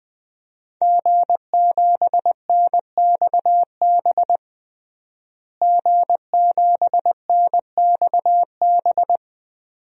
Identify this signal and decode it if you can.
Morse code, heard as a keyed tone, about 20 wpm, 700 Hz. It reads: G7NXB G7NXB
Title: DXCC-100 Call Signs - 5 Characters - Repeated 2x - 20wpm